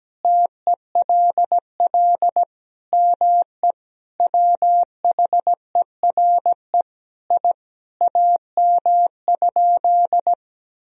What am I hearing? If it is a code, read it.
Morse code, heard as a keyed tone, about 17 wpm, 700 Hz. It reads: TELL ME WHERE I AM?
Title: Sentences from Top 1000 Words - Review Part 2 - 17wpm